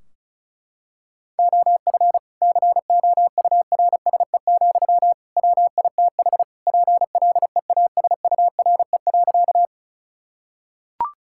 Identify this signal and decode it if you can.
Morse code, heard as a keyed tone, about 36 wpm, 700 Hz. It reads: OF COURSE, WITH PLEASURE.
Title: Sentences from Top 1600 Words 35wpm